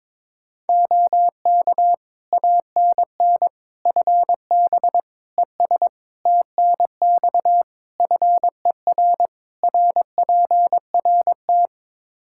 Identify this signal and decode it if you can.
Morse code, heard as a keyed tone, about 22 wpm, 700 Hz. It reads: OK ANN FB ES TNX FER RPRT
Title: K1USN SST QSO Practice 22wpm